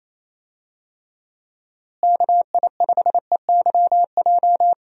Morse code, heard as a keyed tone, about 28 wpm, 700 Hz. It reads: KI5EYJ
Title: USA Call Signs - 6 Characters - Repeated 2x - 28wpm